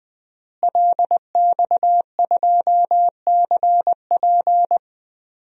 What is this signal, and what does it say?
Morse code, heard as a keyed tone, about 20 wpm, 700 Hz. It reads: LX2CP
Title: DXCC-100 Call Signs - 5 Characters - Repeated 3x - 20wpm